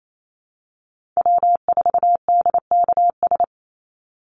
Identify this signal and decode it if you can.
Morse code, heard as a keyed tone, about 28 wpm, 700 Hz. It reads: W4DKS
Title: Call Signs Repeated 3x - 28wpm